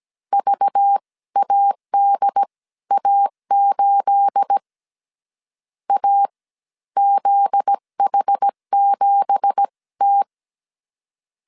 Morse code, heard as a keyed tone, about 17 wpm, spaced as for 13 wpm, 800 Hz. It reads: VADA8 AZH7T
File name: CW / DET / 1 9 1500C S B 16wpm 2856